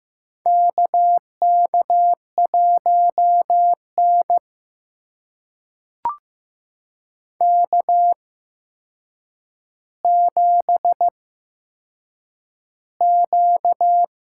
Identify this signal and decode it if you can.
Morse code, heard as a keyed tone, about 15 wpm, spaced as for 7 wpm, 700 Hz. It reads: KK1N K7Q
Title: USA Call Signs - 4 Characters - 8x Character Spacing - 15wpm